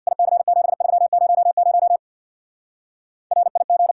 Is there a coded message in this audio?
Yes: Morse code from 0.1 to 3.9 s, heard as a keyed tone, about 66 wpm, 700 Hz. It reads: IC7300 RIG